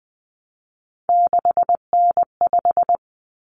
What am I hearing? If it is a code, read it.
Morse code, heard as a keyed tone, about 20 wpm, 700 Hz. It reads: 6N5
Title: Common Cut Numbers - 3-Digit 20wpm